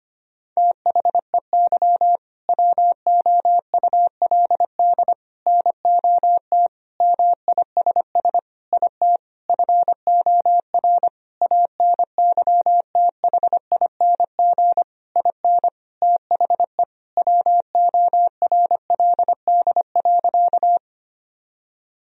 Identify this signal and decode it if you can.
Morse code, heard as a keyed tone, about 25 wpm, 700 Hz. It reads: THEY WOULD NOT MISS IT FOR ANYTHING IN THE WORLD.
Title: Sentences from Top 1000 Words - Review Part 3 - 25wpm